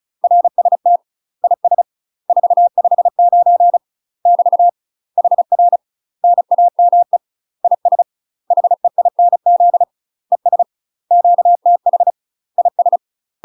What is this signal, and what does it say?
Morse code, heard as a keyed tone, about 35 wpm, 700 Hz. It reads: RST IS 459 <BT> HR NAME IS HEINZ ES QTH IS
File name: UK MASTER QSOs TRAINING / 35WPM / UK MASTER QSOs 0042